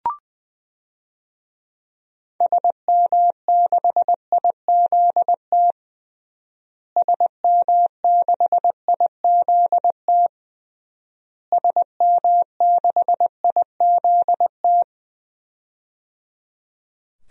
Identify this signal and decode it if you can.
Morse code, heard as a keyed tone, about 20 wpm, 700 Hz. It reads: SM6IZT SM6IZT SM6IZT
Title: DXCC-100 Call Signs - 6 Characters - Repeated 3x - 20wpm